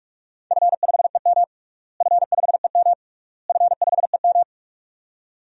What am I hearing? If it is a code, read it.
Morse code, heard as a keyed tone, about 47 wpm, 700 Hz. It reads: F5EM F5EM F5EM
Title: DXCC-100 Call Signs - 4 Characters - Repeated 3x - 45wpm